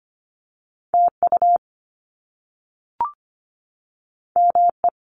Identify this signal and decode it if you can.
Morse code, heard as a keyed tone, about 25 wpm, 700 Hz. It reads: TU ME
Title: Top Two Letter Combinations in Words 25wpm